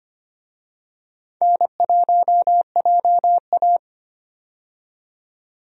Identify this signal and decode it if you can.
Morse code, heard as a keyed tone, about 25 wpm, 700 Hz. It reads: N1JA
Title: USA Call Signs - 4 Characters - 25wpm